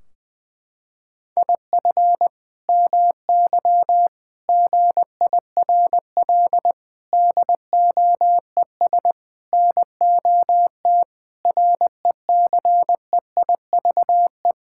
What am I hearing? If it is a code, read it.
Morse code, heard as a keyed tone, about 20 wpm, 700 Hz. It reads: IF MY GIRL DOES NOT RECEIVE
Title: Sentences from Top 1600 Words 20wpm